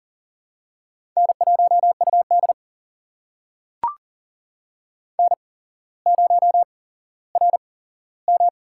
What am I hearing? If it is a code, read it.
Morse code, heard as a keyed tone, about 40 wpm, spaced as for 17 wpm, 700 Hz. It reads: N1UD N0RM
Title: USA Call Signs - 4 Characters - 8x Character Spacing - 40wpm